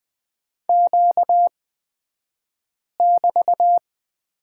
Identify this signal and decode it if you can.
Morse code, heard as a keyed tone, about 20 wpm, spaced as for 5 wpm, 700 Hz. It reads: Q<BT>
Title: 235 Lesson 38d - Letter q - Two Characters 20wpm